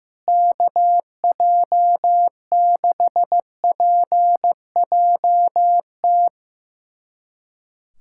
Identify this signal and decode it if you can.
Morse code, heard as a keyed tone, about 15 wpm, 700 Hz. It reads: KJ6PJT
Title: USA Callsign DE Callsign - 6 Characters - 15wpm